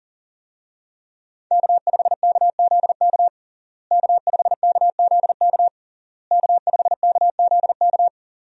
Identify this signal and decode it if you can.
Morse code, heard as a keyed tone, about 37 wpm, 700 Hz. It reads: K5KZK K5KZK K5KZK
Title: USA Call Signs - 5 Characters - Repeated 3x - 40wpm